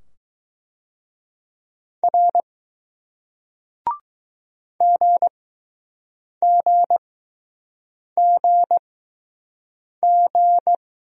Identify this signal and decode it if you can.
Morse code, heard as a keyed tone, about 20 wpm, spaced as for 6 wpm, 700 Hz. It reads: R GGGG